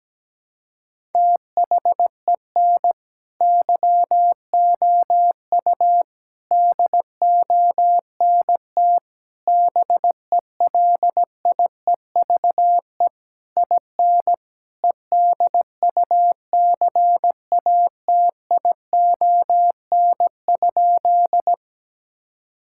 Morse code, heard as a keyed tone, about 17 wpm, 700 Hz. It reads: THEN YOU DONT BELIEVE IN EDUCATION?
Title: Sentences from Top 1000 Words - Review Part 6 - 17wpm